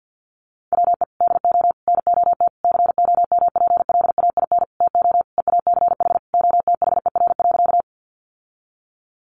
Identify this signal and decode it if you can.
Morse code, heard as a keyed tone, about 50 wpm, 700 Hz. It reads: WE DO NOT COMPLAIN TO EACH OTHER.
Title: Sentences from Top 1300 Words 50wpm